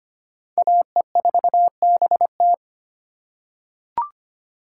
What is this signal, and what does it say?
Morse code, heard as a keyed tone, about 25 wpm, 700 Hz. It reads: AE4BT
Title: USA Call Signs - 5 Characters - 2x Character Spacing - 25wpm